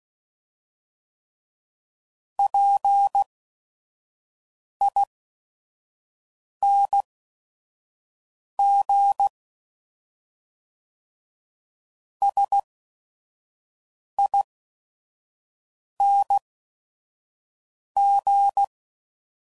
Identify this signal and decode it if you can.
Morse code, heard as a keyed tone, about 16 wpm, spaced as for 5 wpm, 800 Hz. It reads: PING SING
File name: Suffixes / Special ING's / VerySlowToFaster ING's